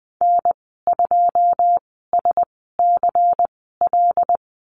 Morse code, heard as a keyed tone, about 19 wpm, spaced as for 14 wpm, 700 Hz. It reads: N2SCL
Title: 147 Lesson 24g - Letter y - Call Signs 20wpm